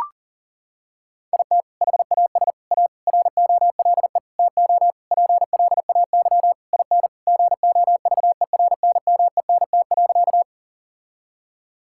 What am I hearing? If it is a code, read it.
Morse code, heard as a keyed tone, about 40 wpm, 700 Hz. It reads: IT HAS A ROLE TO PLAY IN GOVERNMENT.
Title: Sentences from Top 1000 Words - Review Part 6 - 40wpm